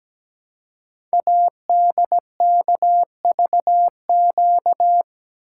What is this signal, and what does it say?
Morse code, heard as a keyed tone, about 17 wpm, 700 Hz. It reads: ADKVQ
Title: Code Groups - 5 Letters - 17wpm